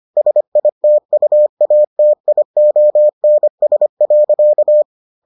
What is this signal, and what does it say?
Morse code, heard as a keyed tone, about 25 wpm, 600 Hz. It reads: SITUATIONS.